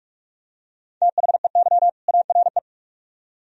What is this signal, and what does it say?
Morse code, heard as a keyed tone, about 44 wpm, 700 Hz. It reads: THEY ARE